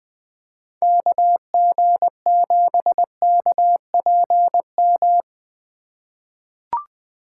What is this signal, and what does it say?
Morse code, heard as a keyed tone, about 21 wpm, 700 Hz. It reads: KG7KPM
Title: USA Call Signs - 6 Characters - 3x Character Spacing - 20wpm